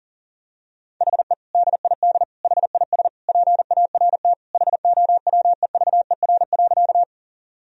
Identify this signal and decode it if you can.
Morse code, heard as a keyed tone, about 40 wpm, 700 Hz. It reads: HE DID HIS PART HOWEVER.
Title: Sentences from Top 1000 Words - Review Part 1 - 40wpm